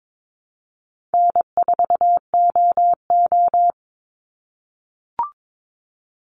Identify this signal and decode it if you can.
Morse code, heard as a keyed tone, about 22 wpm, 700 Hz. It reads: N4OO
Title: Call Signs - 22wpm